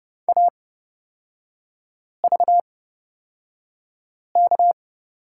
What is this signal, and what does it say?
Morse code, heard as a keyed tone, about 30 wpm, spaced as for 5 wpm, 700 Hz. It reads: AVK
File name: Lesson 36e - BT - Three Characters 30wpm